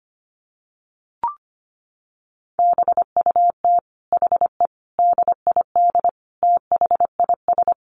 Morse code, heard as a keyed tone, about 25 wpm, 700 Hz. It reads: BUT HE DID THIS
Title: Sentences from Top 1400 Words 25wpm